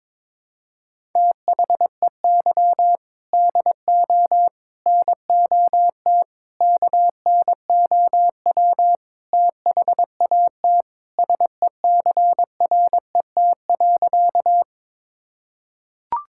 Morse code, heard as a keyed tone, about 21 wpm, 700 Hz. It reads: THEY DO NOT KNOW THAT SECRET.
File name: practice / 22wpm / Sentences with 4-Word Phrases - 22wpm